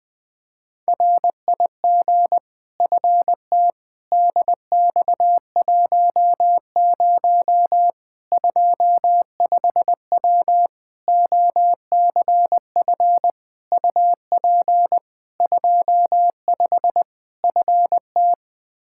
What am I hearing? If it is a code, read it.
Morse code, heard as a keyed tone, about 20 wpm, 700 Hz. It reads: RIG FT DX10 25W OCF UP 25 FT
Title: Practice QSO - Exchange 2 - Info - 20wpm